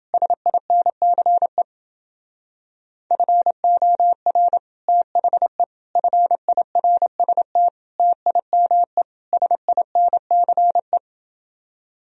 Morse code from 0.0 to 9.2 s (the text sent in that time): SINCE FOR THE FIRST TIME